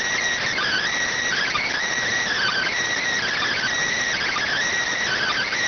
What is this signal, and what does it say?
Emergency Alert System (SAME) tones, heard as harsh buzzing bursts. It is heard from 0.5 to 4.5 s.